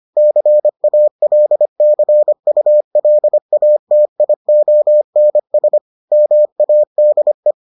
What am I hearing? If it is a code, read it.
Morse code, heard as a keyed tone, about 25 wpm, 600 Hz. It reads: CALCULATIONS MADE